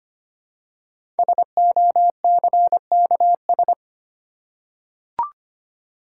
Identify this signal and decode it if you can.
Morse code, heard as a keyed tone, about 25 wpm, 700 Hz. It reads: SOCKS
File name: practice / 25wpm / Easy Words with Hints - 25wpm